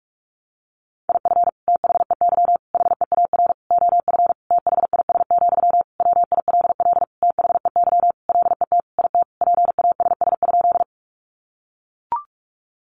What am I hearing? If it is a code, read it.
Morse code, heard as a keyed tone, about 44 wpm, 700 Hz. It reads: IF THEY HEAR OF THIS, WILL THEY LET IT PASS?
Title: Sentences from Top 400 Words 45wpm